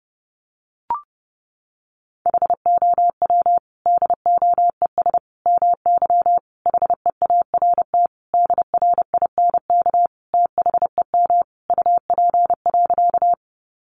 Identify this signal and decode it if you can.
Morse code, heard as a keyed tone, about 30 wpm, 700 Hz. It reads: HOW DOES MY HEART DRINK THEM UP.
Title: Sentences from Top 1500 Words 30wpm